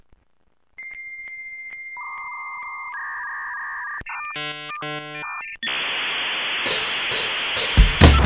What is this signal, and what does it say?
Dial-up modem handshake.